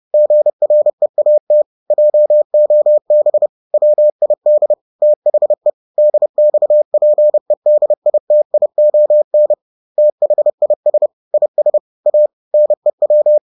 Morse code, heard as a keyed tone, about 30 wpm, 600 Hz. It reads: GREAT JOB WID THE DXPEDITION THIS IS A NEW